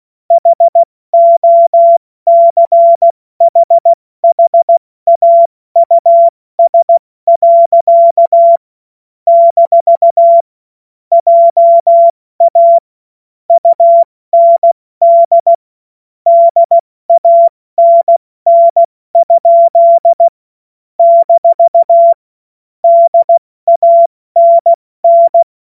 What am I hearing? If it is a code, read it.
Morse code, heard as a keyed tone, about 16 wpm, 680 Hz. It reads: HOCHHAUS. - JA UND DANN? - DANN
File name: Morsekurs / Witze / Witz 12 März 2024 70 F80